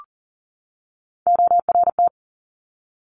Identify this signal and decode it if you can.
Morse code, heard as a keyed tone, about 41 wpm, 690 Hz. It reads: ORT
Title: Top Three Letter Combinations in Words 40wpm